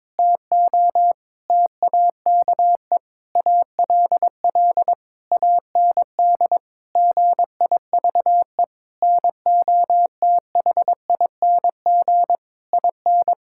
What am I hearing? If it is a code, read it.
Morse code, heard as a keyed tone, about 22 wpm, 700 Hz. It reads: TO TAKE ALL AND GIVE NOTHING IN
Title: Sentences from Top 1000 Words - Review Part 6 - 22wpm